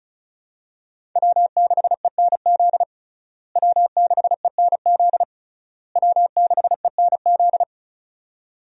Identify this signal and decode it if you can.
Morse code, heard as a keyed tone, about 36 wpm, 700 Hz. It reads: W6ENZ W6ENZ W6ENZ
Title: USA Call Signs - 5 Characters - Repeated 3x - 35wpm